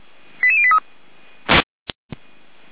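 Five-tone selective call (ZVEI), heard as quick runs of five tones, tones of about 70 ms. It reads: 88083
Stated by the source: ZVEI five-tone call received by SDR.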